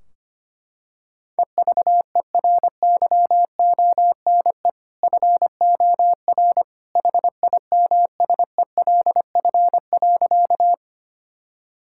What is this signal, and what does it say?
Morse code, heard as a keyed tone, about 25 wpm, 700 Hz. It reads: EVERYONE FOR HIMSELF.
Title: Sentences from Top 1600 Words 25wpm